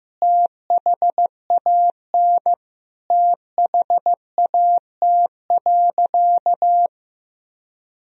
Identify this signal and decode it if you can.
Morse code, heard as a keyed tone, about 16 wpm, 700 Hz. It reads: THAN THAT.
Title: Sentences from Top 1000 Words - Review Part 3 - 15wpm